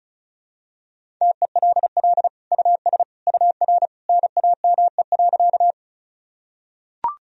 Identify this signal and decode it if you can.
Morse code, heard as a keyed tone, about 36 wpm, 700 Hz. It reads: TELL US UR NAME.
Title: Sentences from Top 200 Words and Abbreviations 35wpm